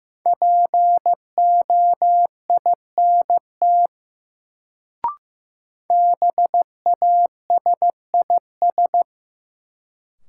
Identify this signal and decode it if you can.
Morse code, heard as a keyed tone, about 15 wpm, 700 Hz. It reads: POINT BASIS
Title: Sets of 3 Words - New Word Drill - 15wpm